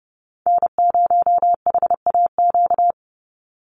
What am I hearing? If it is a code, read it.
Morse code, heard as a keyed tone, about 30 wpm, 700 Hz. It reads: N0HAQ
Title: USA Call Signs 30wpm